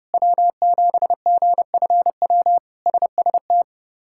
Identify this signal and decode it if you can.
Morse code, heard as a keyed tone, about 30 wpm, 700 Hz. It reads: W7GFW SST
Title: K1USN SST QSO Practice - 30wpm